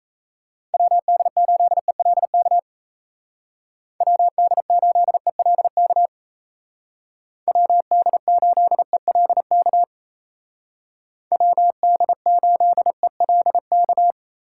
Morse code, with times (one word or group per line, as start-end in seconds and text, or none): 0.7-2.6 WD8ELK
4.0-6.1 WD8ELK
7.5-9.8 WD8ELK
11.3-14.1 WD8ELK